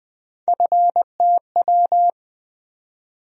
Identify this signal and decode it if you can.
Morse code, heard as a keyed tone, about 20 wpm, 700 Hz. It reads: FTW
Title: USA Call Sign Postfixes - 3 Letters 20wpm